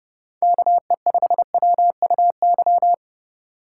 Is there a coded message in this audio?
Yes: Morse code from 0.4 to 2.9 s, heard as a keyed tone, about 29 wpm, 700 Hz. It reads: KE5WUY